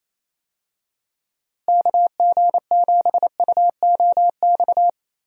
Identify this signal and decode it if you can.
Morse code, heard as a keyed tone, about 28 wpm, 700 Hz. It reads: KG7UOX
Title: USA Call Signs - 6 Characters - Repeated 2x - 28wpm